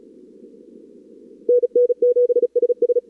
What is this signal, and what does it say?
Morse code, heard as a keyed tone, about 40 wpm, 440 Hz. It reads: NN7SS
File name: CWSprint / Feb2022 / k5tr-feb-na-sprint-cw-k5ot -0100